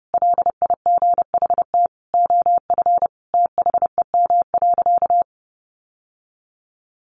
Morse code, heard as a keyed tone, about 30 wpm, 700 Hz. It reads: LIGHT OF THEM.
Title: Sentences from Top 1300 Words 30wpm